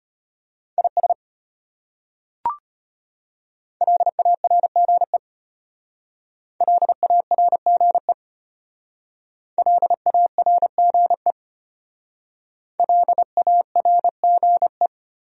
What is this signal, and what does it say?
Morse code, heard as a keyed tone, about 33 wpm, 700 Hz. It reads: IS LARGE LARGE LARGE LARGE